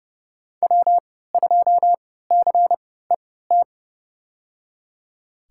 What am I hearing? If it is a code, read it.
Morse code, heard as a keyed tone, about 30 wpm, spaced as for 17 wpm, 700 Hz. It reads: W2CET
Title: USA Call Signs - 5 Characters - 3x Character Spacing - 30wpm